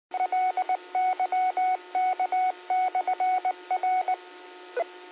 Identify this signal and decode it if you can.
Morse code, heard as a keyed tone, about 19 wpm, 740 Hz. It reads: LYK/R